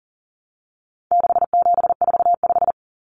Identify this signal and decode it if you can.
Morse code, heard as a keyed tone, about 40 wpm, 700 Hz. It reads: 6745
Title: Common Cut Numbers - 4-Digit 40wpm